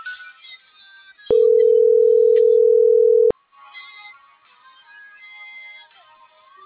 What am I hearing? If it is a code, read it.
Ringback tone (North America), 440 + 480 Hz.